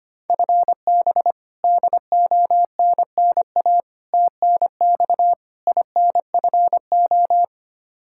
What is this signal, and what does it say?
Morse code, heard as a keyed tone, about 25 wpm, 700 Hz. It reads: FB DONNA TNX INFO
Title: Practice QSO - Exchange 3 - More Info - 25wpm